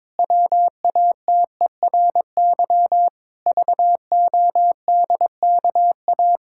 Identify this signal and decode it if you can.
Morse code, heard as a keyed tone, about 22 wpm, 700 Hz. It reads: WATERY VODKA
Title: Pangram Challenge v2 - 22wpm